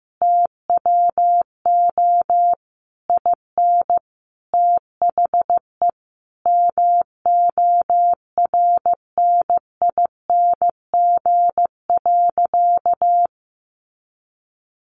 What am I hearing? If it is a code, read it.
Morse code, heard as a keyed tone, about 15 wpm, 700 Hz. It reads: TWO IN THE MORNING.
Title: Sentences from Top 1300 Words 15wpm